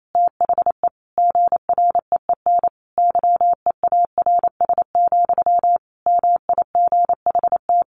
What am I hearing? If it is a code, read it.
Morse code, heard as a keyed tone, about 28 wpm, 700 Hz. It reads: THE GREEN YEARS, MIGHT
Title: A Sound Of Thunder 28wpm